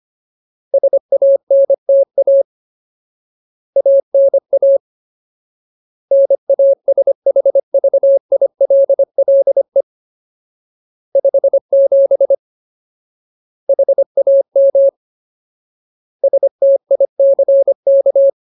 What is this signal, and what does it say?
Morse code, heard as a keyed tone, about 25 wpm, 550 Hz. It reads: SANTA ANA NASHVILLE 57 HAM STICK
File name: Ultimate Word List / 25WPM Ultimate